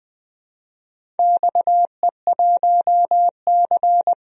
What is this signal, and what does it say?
Morse code, heard as a keyed tone, about 20 wpm, 700 Hz. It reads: XE1C
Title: DXCC-100 Call Signs - 4 Characters - 20wpm